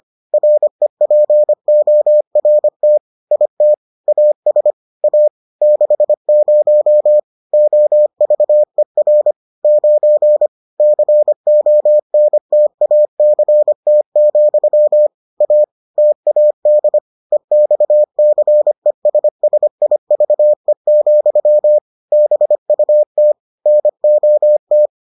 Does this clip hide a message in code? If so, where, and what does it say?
Yes: Morse code from 0.3 to 24.9 s, heard as a keyed tone, about 25 wpm, 600 Hz. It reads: REPORT IT AS A 60 OVER 9 CONTACT, A TAD EXCESSIVE, BUT NOT